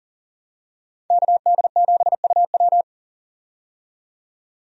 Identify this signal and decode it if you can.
Morse code, heard as a keyed tone, about 40 wpm, 700 Hz. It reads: KD7UW
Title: USA Call Signs - 5 Characters - 40wpm